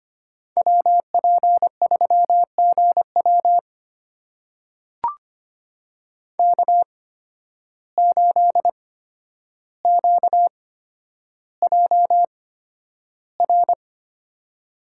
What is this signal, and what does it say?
Morse code, heard as a keyed tone, about 25 wpm, spaced as for 11 wpm, 700 Hz. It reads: WP3GW K8QJR
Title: USA Call Signs - 5 Characters - 8x Character Spacing - 25wpm